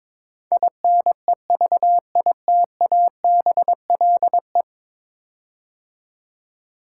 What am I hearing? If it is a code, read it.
Morse code, heard as a keyed tone, about 22 wpm, 700 Hz. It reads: INEVITABLE
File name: practice / 22wpm / 10 Letter Words - First 3 Letters Given - 22wpm